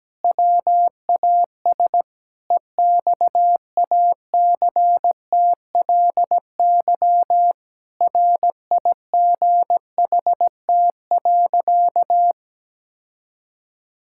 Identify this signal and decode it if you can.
Morse code, heard as a keyed tone, about 17 wpm, 700 Hz. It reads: WAS EXACTLY RIGHT.
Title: Sentences from Top 1000 Words - Review Part 6 - 17wpm